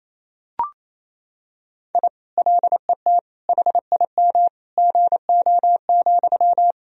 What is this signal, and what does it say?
Morse code, heard as a keyed tone, about 28 wpm, 700 Hz. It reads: I LET HIM GO,
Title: Sentences from Top 1000 Words - Review Part 2 - 28wpm